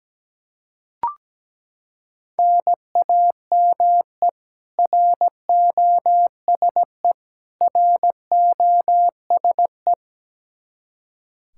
Morse code, heard as a keyed tone, about 17 wpm, 700 Hz. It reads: NAME ROSE ROSE
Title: Practice QSO - Exchange 1 - Beginning - 17wpm